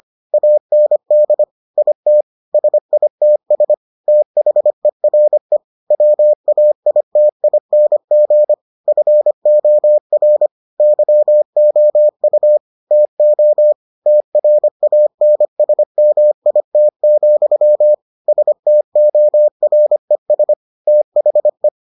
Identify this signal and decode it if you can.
Morse code, heard as a keyed tone, about 25 wpm, 600 Hz. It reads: AND IT SITS THERE WAITING FOR YOU TO TRANSMIT, STORES THE